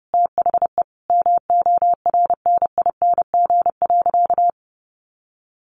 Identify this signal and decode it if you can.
Morse code, heard as a keyed tone, about 30 wpm, 700 Hz. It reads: THE MORNING.
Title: Sentences from Top 1300 Words 30wpm